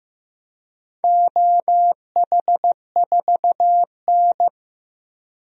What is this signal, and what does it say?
Morse code, heard as a keyed tone, about 15 wpm, 700 Hz. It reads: OH4N
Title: DXCC-100 Call Signs - 4 Characters - 15wpm